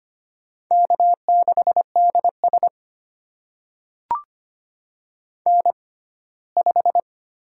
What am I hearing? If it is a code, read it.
Morse code, heard as a keyed tone, about 25 wpm, spaced as for 17 wpm, 700 Hz. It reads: K6DS N5
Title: USA Call Signs - 4 Characters - 6x Character Spacing - 25wpm